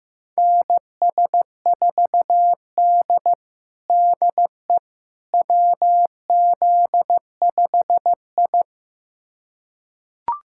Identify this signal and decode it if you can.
Morse code, heard as a keyed tone, about 15 wpm, 700 Hz. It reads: NS4D DE WZ5I
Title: USA Callsign DE Callsign - 4 Characters - 15wpm